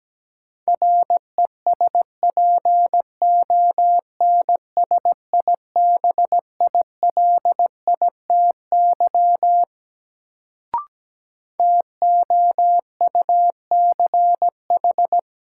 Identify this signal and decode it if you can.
Morse code, heard as a keyed tone, about 17 wpm, 700 Hz. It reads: RESPONSIBILITY TOUCH
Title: Sentences from Top 1200 Words - New Word Drill - 17wpm